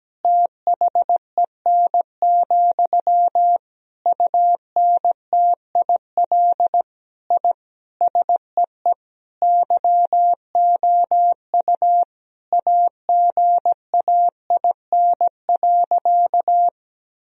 Morse code, heard as a keyed tone, about 17 wpm, 700 Hz. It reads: THEN, UNTIL I SEE YOU AGAIN.
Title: Sentences from Top 1000 Words - Review Part 1 - 17wpm